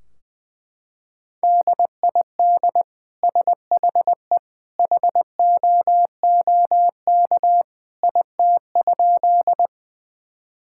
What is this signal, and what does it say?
Morse code, heard as a keyed tone, about 20 wpm, 700 Hz. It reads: DID SHE HOOK IT?
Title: Sentences from Top 1600 Words 20wpm